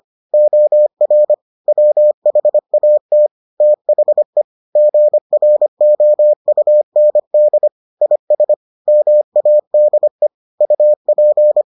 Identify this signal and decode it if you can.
Morse code, heard as a keyed tone, about 25 wpm, 600 Hz. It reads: OR WHAT THE GROUND IS MADE UP